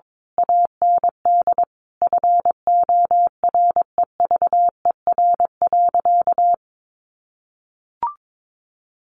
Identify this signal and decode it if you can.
Morse code, heard as a keyed tone, about 22 wpm, 700 Hz. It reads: AND FOREVER.
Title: A Sound Of Thunder 22wpm